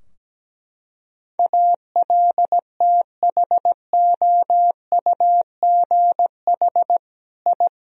Morse code, heard as a keyed tone, about 17 wpm, 700 Hz. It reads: ALTHOUGH I
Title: Sentences from Top 1000 Words - Review Part 3 - 17wpm